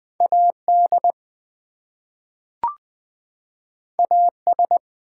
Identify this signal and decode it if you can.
Morse code, heard as a keyed tone, about 20 wpm, 700 Hz. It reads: AD AS
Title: Two Letter Words - 20wpm